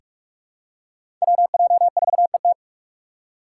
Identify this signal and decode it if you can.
Morse code, heard as a keyed tone, about 43 wpm, 700 Hz. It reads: WJ4ET
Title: USA Call Signs - 5 Characters - 6x Character Spacing - 45wpm